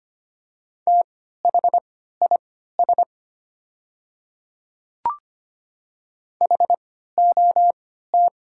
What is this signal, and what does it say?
Morse code, heard as a keyed tone, about 25 wpm, spaced as for 14 wpm, 700 Hz. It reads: THIS HOT
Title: 3x Character Spacing - Top 100 Words - 25wpm